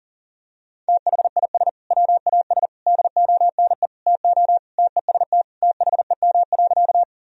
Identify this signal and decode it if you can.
Morse code, heard as a keyed tone, about 40 wpm, 700 Hz. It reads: THIS WAS DONE TO TEST THEM.
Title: Sentences from Top 1000 Words - Review Part 3 - 40wpm